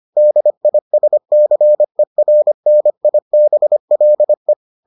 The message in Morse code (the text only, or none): DISCERNIBLE